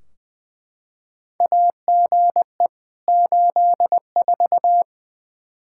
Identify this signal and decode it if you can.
Morse code, heard as a keyed tone, about 20 wpm, 700 Hz. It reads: AGE 84
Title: Practice QSO - Exchange 3 - More Info - 20wpm